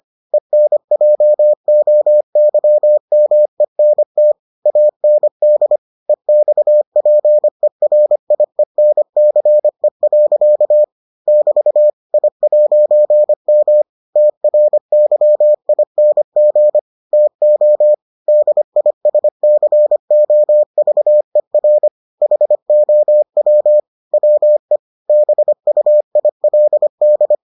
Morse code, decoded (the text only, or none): ENJOYMENT AND EXPERIENCE. <BT> I'M TRYING TO DISCOVER HOW WE BUILD